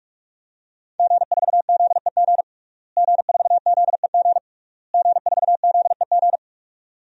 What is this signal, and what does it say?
Morse code, heard as a keyed tone, about 48 wpm, 700 Hz. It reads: G4ZEG G4ZEG G4ZEG
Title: DXCC-100 Call Signs - 5 Characters - Repeated 3x - 45wpm